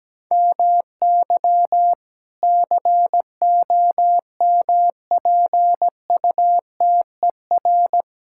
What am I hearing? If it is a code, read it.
Morse code, heard as a keyed tone, about 17 wpm, 700 Hz. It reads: MY COMPUTER
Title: Sentences from Top 1000 Words - Review Part 6 - 17wpm